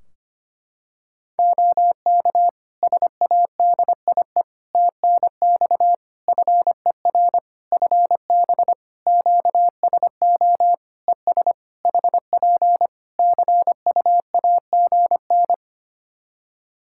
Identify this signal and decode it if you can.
Morse code, heard as a keyed tone, about 25 wpm, 700 Hz. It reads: OK SADIE TNX FER FB QSO ES HP CUAGN
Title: Practice QSO - Exchange 4 - Ending - 25wpm